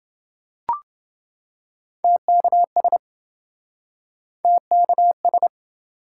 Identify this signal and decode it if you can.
Morse code, heard as a keyed tone, about 29 wpm, 700 Hz. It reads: TKS TKS